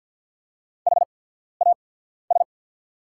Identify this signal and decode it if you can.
Morse code, heard as a keyed tone, about 48 wpm, spaced as for 14 wpm, 690 Hz. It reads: HAS
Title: 8x Character Spacing - Top 100 Words - 50wpm